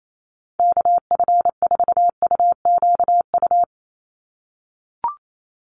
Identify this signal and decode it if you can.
Morse code, heard as a keyed tone, about 28 wpm, 700 Hz. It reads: KF4UQU
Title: USA Call Signs 28wpm